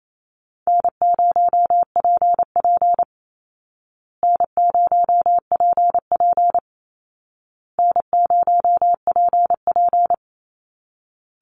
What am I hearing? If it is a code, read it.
Morse code, heard as a keyed tone, about 28 wpm, 700 Hz. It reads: N0PP N0PP N0PP
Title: Easy Call Signs Repeated 3x - 28wpm